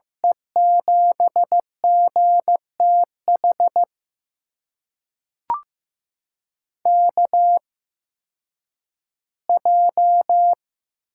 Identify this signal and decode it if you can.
Morse code, heard as a keyed tone, about 15 wpm, spaced as for 10 wpm, 700 Hz. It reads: E7GTH KJ
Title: USA Call Signs - 6 Characters - 8x Character Spacing - 15wpm